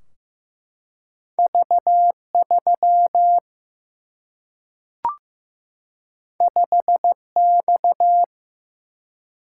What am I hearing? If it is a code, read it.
Morse code, heard as a keyed tone, about 15 wpm, 700 Hz. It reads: V3 5X
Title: Single-Letter Single-Number 15wpm